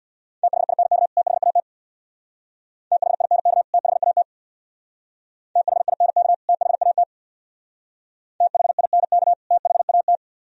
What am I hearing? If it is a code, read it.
Morse code, heard as a keyed tone, about 62 wpm, 700 Hz. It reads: THINK THAT THINK THAT THINK THAT THINK THAT
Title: Speed-Racing - Sets of 2 Words 50wpm